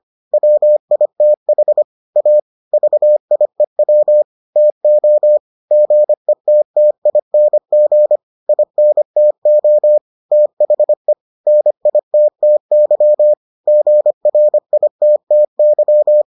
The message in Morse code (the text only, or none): WITH A VIEW TO GETTING INTO THE NITTY GRITTY